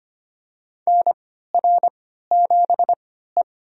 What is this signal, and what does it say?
Morse code, heard as a keyed tone, about 25 wpm, spaced as for 14 wpm, 700 Hz. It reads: NR7E
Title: USA Call Signs - 4 Characters - 3x Character Spacing - 25wpm